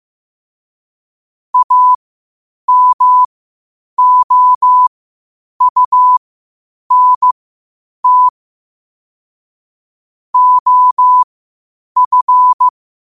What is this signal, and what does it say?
Morse code, heard as a keyed tone, about 15 wpm, spaced as for 8 wpm, 1000 Hz. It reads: AMOUNT OF
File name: qotdfiles / 8wpm